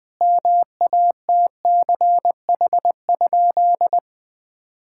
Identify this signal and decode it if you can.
Morse code, heard as a keyed tone, about 21 wpm, 700 Hz. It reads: MATCH?
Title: Sentences from Top 1600 Words 20wpm